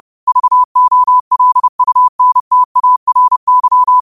Morse code, heard as a keyed tone, about 30 wpm, 1000 Hz. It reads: VOLUNTARY
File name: qotdfiles / 30wpm